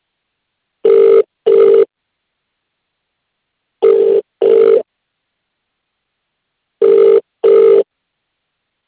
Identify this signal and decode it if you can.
Ringback tone (UK), 400 + 450 Hz.